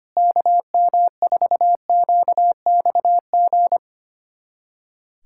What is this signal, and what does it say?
Morse code, heard as a keyed tone, about 25 wpm, 700 Hz. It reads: KM4QXG
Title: USA Callsign DE Callsign - 6 Characters - 25wpm